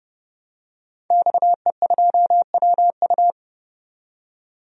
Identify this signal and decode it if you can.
Morse code, heard as a keyed tone, about 30 wpm, 700 Hz. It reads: XE2WU
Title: DXCC-100 Call Signs - 5 Characters - 30wpm